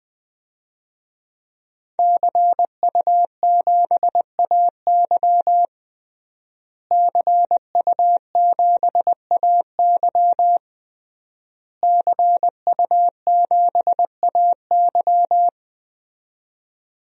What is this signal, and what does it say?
Morse code, heard as a keyed tone, about 21 wpm, 700 Hz. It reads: CU7AY CU7AY CU7AY
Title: DXCC-100 Call Signs - 5 Characters - Repeated 3x - 20wpm